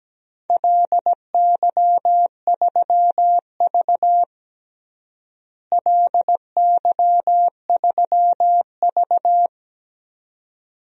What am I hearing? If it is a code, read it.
Morse code, heard as a keyed tone, about 17 wpm, 700 Hz. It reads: LY3V LY3V
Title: DXCC-100 Call Signs - 4 Characters - Repeated 2x - 17wpm